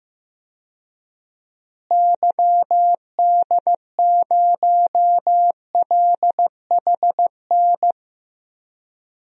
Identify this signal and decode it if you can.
Morse code, heard as a keyed tone, about 15 wpm, 700 Hz. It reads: YD0LHN
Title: DXCC-100 Call Signs - 6 Characters - Repeated 2x - 15wpm